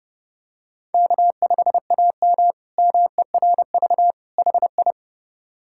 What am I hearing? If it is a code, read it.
Morse code, heard as a keyed tone, about 30 wpm, 700 Hz. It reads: K5AM MERV HI
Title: K1USN SST QSO Practice - 30wpm